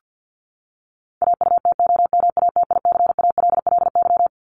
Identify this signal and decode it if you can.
Morse code, heard as a keyed tone, about 50 wpm, 700 Hz. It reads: AUTOMATICALLY
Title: Top 1751-2000 Words 50wpm